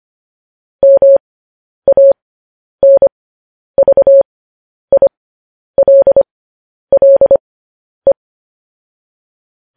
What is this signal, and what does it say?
Morse code, heard as a keyed tone, about 25 wpm, spaced as for 10 wpm, 550 Hz. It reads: MANVILLE
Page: news headlines in morse code at 10 wpm